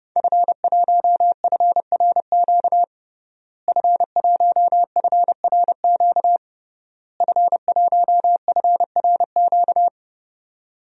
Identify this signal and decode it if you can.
Morse code, heard as a keyed tone, about 32 wpm, 700 Hz. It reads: F1FRQ F1FRQ F1FRQ
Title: DXCC-100 Call Signs - 5 Characters - Repeated 3x - 30wpm